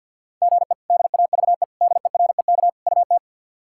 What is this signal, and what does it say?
Morse code, heard as a keyed tone, about 53 wpm, 690 Hz. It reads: GE DAVE DEREK UT
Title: K1USN SST QSO Practice - 50wpm